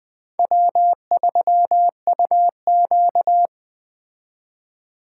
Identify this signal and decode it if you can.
Morse code, heard as a keyed tone, about 21 wpm, 700 Hz. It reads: W3UQ